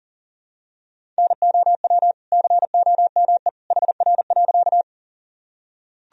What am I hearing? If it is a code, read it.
Morse code, heard as a keyed tone, about 37 wpm, 700 Hz. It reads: NOW COME HR.
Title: Sentences from Top 200 Words and Abbreviations 40wpm